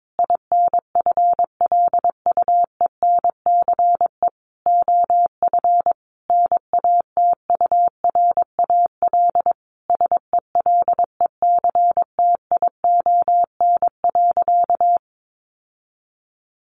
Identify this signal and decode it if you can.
Morse code, heard as a keyed tone, about 22 wpm, 700 Hz. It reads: INFLUENCE OF NATURAL SELECTION.
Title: Sentences from Top 1400 Words 22wpm